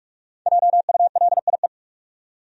Morse code, heard as a keyed tone, about 42 wpm, 700 Hz. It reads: JULIE
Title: USA Common Names - Max 5 Letters - 45wpm